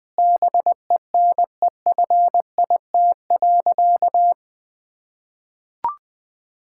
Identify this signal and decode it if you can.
Morse code, heard as a keyed tone, about 21 wpm, 700 Hz. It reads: BENEFIT.
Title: Sentences from Top 1000 Words - Review Part 6 - 20wpm